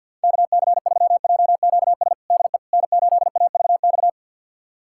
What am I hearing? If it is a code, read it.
Morse code, heard as a keyed tone, about 50 wpm, 700 Hz. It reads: KC3JQS DE N8AVX
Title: USA Callsign DE Callsign 50wpm